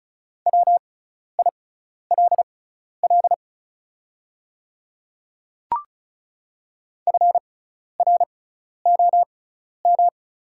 Morse code, heard as a keyed tone, about 34 wpm, spaced as for 12 wpm, 700 Hz. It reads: WILL FROM
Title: 6x Character Spacing - Top 100 Words - 35wpm